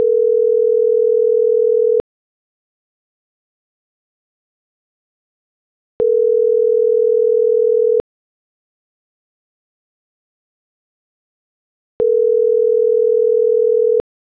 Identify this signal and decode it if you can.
Ringback tone (North America), 440 + 480 Hz.
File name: CENTRAL SKYBLUE / out alonso.cruz - .98782